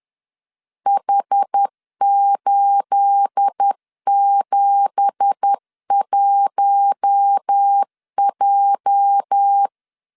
Morse code, heard as a keyed tone, about 10 wpm, 800 Hz. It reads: H871J